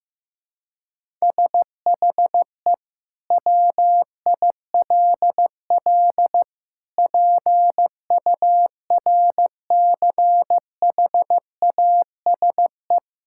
Morse code, heard as a keyed tone, about 15 wpm, 700 Hz. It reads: SHE WILL PURCHASE
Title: Sentences from Top 1000 Words - Review Part 5 - 15wpm